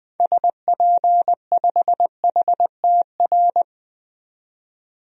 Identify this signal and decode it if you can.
Morse code, heard as a keyed tone, about 20 wpm, 700 Hz. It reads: SP5HTR
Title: DXCC-100 Call Signs - 6 Characters - 20wpm